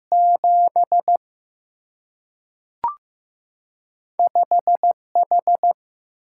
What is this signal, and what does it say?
Morse code, heard as a keyed tone, about 15 wpm, 700 Hz. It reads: 7 5 H
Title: Single-Letter Single-Number 15wpm